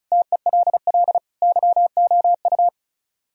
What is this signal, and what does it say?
Morse code, heard as a keyed tone, about 45 wpm, 700 Hz. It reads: TELL YOU